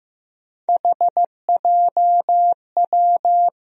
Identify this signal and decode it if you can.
Morse code, heard as a keyed tone, about 15 wpm, 700 Hz. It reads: HJW
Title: USA Call Sign Postfixes - 3 Letters 15wpm